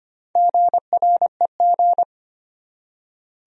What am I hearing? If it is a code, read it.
Morse code, heard as a keyed tone, about 25 wpm, 700 Hz. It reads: GREG
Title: USA Common Names - Max 4 Letters - 25wpm